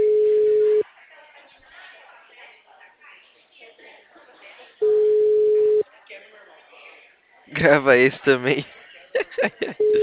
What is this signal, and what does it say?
Ringback tone (Europe, 425 Hz), with speech.